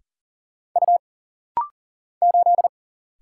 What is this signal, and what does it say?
Morse code, heard as a keyed tone, about 40 wpm, 700 Hz. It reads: U 8
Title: Single Letter-Number - Rapid-Fire 40wpm